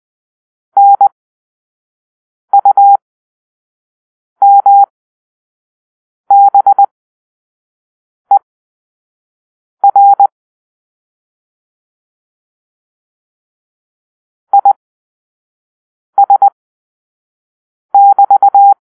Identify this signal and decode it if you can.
Morse code, heard as a keyed tone, about 20 wpm, spaced as for 5 wpm, 800 Hz. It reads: NUMBER IS<BT>